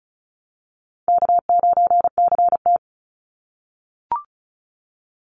Morse code, heard as a keyed tone, about 34 wpm, 700 Hz. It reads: K9CT
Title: Call Signs - 35wpm